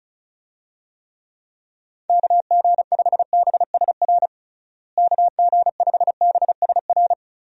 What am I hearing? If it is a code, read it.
Morse code, heard as a keyed tone, about 34 wpm, 700 Hz. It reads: KG5BSR KG5BSR
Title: USA Call Signs - 6 Characters - Repeated 3x - 35wpm